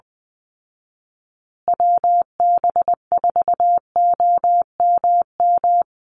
Morse code, heard as a keyed tone, about 20 wpm, 700 Hz. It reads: WB4OMM
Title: Call Signs Repeated 3x - 20wpm